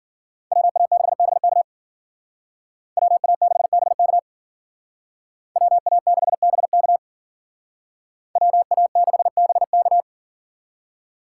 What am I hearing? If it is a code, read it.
Morse code, heard as a keyed tone, about 52 wpm, 700 Hz. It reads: WA6BK WA6BK WA6BK WA6BK